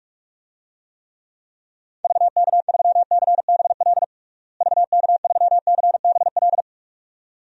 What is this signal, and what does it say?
Morse code, heard as a keyed tone, about 46 wpm, 700 Hz. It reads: VK3CBL VK3CBL
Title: DXCC-100 Call Signs - 6 Characters - Repeated 2x - 45wpm